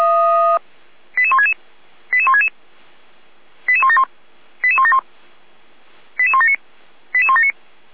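Five-tone selective call (ZVEI), heard as quick runs of five tones, tones of about 70 ms. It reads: 88166 88166 88171 88171 88179 88179
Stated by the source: Received on an SDR.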